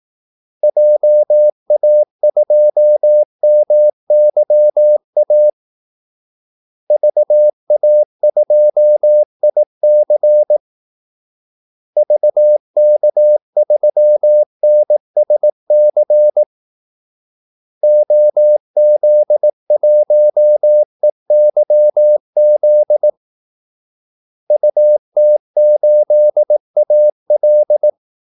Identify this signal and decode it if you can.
Morse code, heard as a keyed tone, about 18 wpm, 600 Hz. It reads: JA2MYA VA2IC VK3NSC OZ1EYZ UT8AL
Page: Kaldesignaler 18 wpm